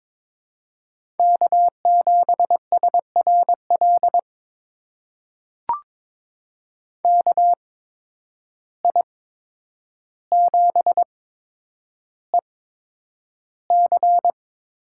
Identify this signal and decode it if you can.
Morse code, heard as a keyed tone, about 21 wpm, spaced as for 9 wpm, 700 Hz. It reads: K7SRL KI7EC
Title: USA Call Signs - 5 Characters - 8x Character Spacing - 22wpm